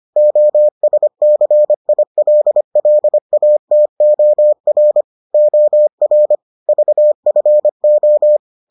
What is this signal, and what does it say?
Morse code, heard as a keyed tone, about 25 wpm, 600 Hz. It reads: OSCILLATOR OR VFO